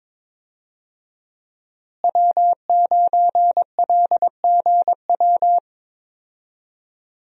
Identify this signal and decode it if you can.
Morse code, heard as a keyed tone, about 22 wpm, 700 Hz. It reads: W9LGW
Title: USA Call Signs - 5 Characters - Repeated 2x - 22wpm